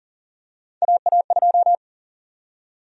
Morse code, heard as a keyed tone, about 40 wpm, 700 Hz. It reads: AA2
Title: USA Call Sign Prefixes - Repeated 2x - 40wpm